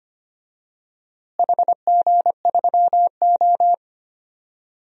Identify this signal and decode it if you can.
Morse code, heard as a keyed tone, about 25 wpm, 700 Hz. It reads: HG3O
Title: DXCC-100 Call Signs - 4 Characters - Repeated 3x - 25wpm